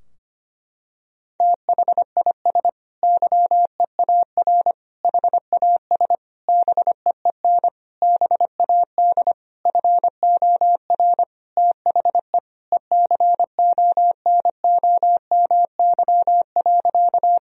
Morse code, heard as a keyed tone, about 25 wpm, 700 Hz. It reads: THIS YEAR HAS BEEN BAD FOR THE ECONOMY.
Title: Sentences from Top 1000 Words - Review Part 3 - 25wpm